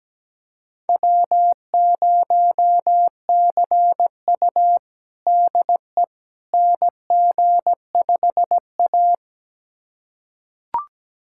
Morse code, heard as a keyed tone, about 17 wpm, 700 Hz. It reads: W0CU DE NG5A
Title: USA Callsign DE Callsign - 4 Characters - 17wpm